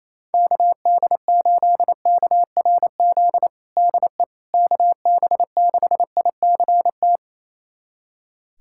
Morse code, heard as a keyed tone, about 27 wpm, 700 Hz. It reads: KD8KRZ DE KB6ICT
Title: USA Callsign DE Callsign 28wpm